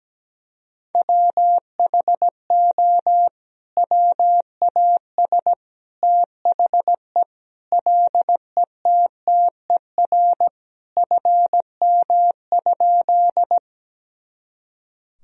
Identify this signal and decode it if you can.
Morse code, heard as a keyed tone, about 17 wpm, 700 Hz. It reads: WHO WAS THE LETTER FM?
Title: Sentences from Top 300 Words and Abbreviations 17wpm